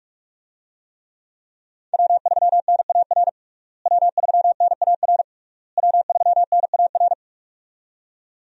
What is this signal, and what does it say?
Morse code, heard as a keyed tone, about 41 wpm, 700 Hz. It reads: W3NAR W3NAR W3NAR
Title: USA Call Signs - 5 Characters - Repeated 3x - 45wpm